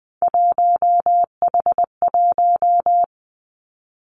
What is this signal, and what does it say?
Morse code, heard as a keyed tone, about 20 wpm, 700 Hz. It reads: 1H1
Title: Three Challenging Letter-Number Combinations 20wpm